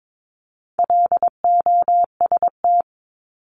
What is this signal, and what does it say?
Morse code, heard as a keyed tone, about 22 wpm, 700 Hz. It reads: LOST
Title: Top 1251-1500 Words 22wpm